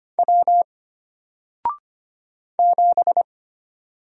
Morse code, heard as a keyed tone, about 25 wpm, 700 Hz. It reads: W 7
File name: Single-Letter-Number-25wpm